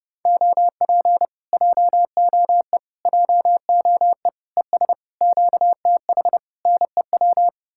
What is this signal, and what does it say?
Morse code, heard as a keyed tone, about 33 wpm, 700 Hz. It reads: OP JOE JOE ES QTH NEW